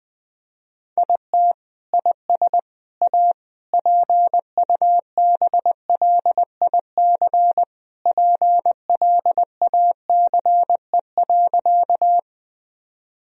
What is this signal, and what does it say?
Morse code, heard as a keyed tone, about 21 wpm, 700 Hz. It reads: IT IS A PUBLIC PLACE.
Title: Sentences from Top 1000 Words - Review Part 3 - 20wpm